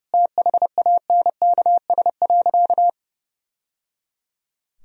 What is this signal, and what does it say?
Morse code, heard as a keyed tone, about 28 wpm, 700 Hz. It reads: THANKS.
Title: Sentences from Top 1000 Words - Review Part 3 - 30wpm